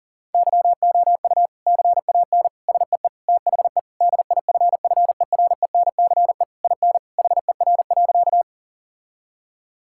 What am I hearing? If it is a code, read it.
Morse code, heard as a keyed tone, about 40 wpm, 700 Hz. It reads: YOU CAN SEE THE DIFFERENCE IN HER.
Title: Sentences from Top 1000 Words - Review Part 5 - 40wpm